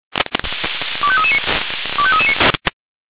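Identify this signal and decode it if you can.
Five-tone selective call (ZVEI), heard as quick runs of five tones, tones of about 70 ms. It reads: 25339 25339
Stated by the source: Five-tone ZVEI selective call 25233, received on SDR.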